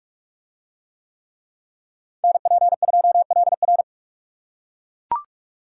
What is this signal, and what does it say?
Morse code, heard as a keyed tone, about 41 wpm, 700 Hz. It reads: NP2LR